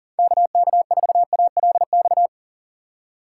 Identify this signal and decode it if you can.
Morse code, heard as a keyed tone, about 40 wpm, 700 Hz. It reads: KK4ALX
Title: USA Call Signs - 6 Characters - 3x Character Spacing - 40wpm